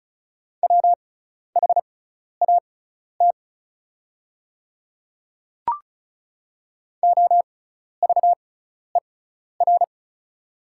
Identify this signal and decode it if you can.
Morse code, heard as a keyed tone, about 34 wpm, spaced as for 12 wpm, 700 Hz. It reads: WHAT OVER
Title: 6x Character Spacing - Top 100 Words - 35wpm